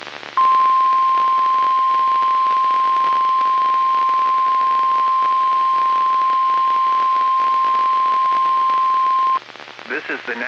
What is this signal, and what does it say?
Emergency Alert System (SAME) tones (the 1050 Hz attention tone), with speech. It is heard from 0.4 to 9.4 s.